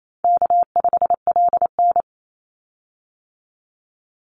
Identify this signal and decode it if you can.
Morse code, heard as a keyed tone, about 28 wpm, 700 Hz. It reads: K5LN
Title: Call Signs - 28wpm